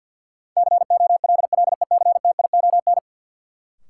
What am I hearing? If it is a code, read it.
Morse code, heard as a keyed tone, about 49 wpm, 700 Hz. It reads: COLLECTION
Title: Sentences from Top 1200 Words - New Word Drill - 50wpm